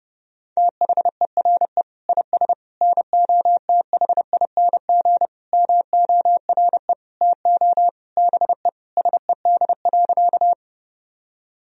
Morse code, heard as a keyed tone, about 32 wpm, 700 Hz. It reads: THERE IS NOTHING MORE TO BE SED.
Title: Sentences from Top 500 Words and Abbreviations 30wpm